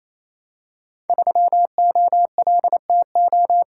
Morse code, heard as a keyed tone, about 27 wpm, 700 Hz. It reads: 3OLTO
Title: Code Groups - 5 Letter:Numbers - 28wpm